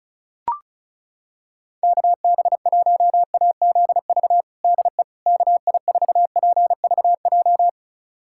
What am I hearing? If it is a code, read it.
Morse code, heard as a keyed tone, about 36 wpm, 700 Hz. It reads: KB1AZV DE KI4PVJ
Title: USA Callsign DE Callsign - 6 Characters - 35wpm